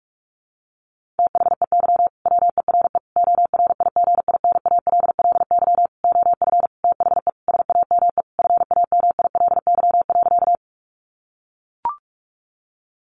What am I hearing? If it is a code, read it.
Morse code, heard as a keyed tone, about 44 wpm, 700 Hz. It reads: THEY WERE ORIGINALLY OF THE SAME FAMILY.
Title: Sentences from Top 1500 Words 45wpm